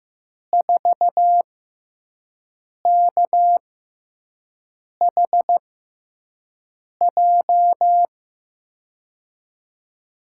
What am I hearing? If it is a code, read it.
Morse code, heard as a keyed tone, about 15 wpm, spaced as for 5 wpm, 700 Hz. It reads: 4KHJ
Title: USA Call Signs - 6x Character Spacing - 15wpm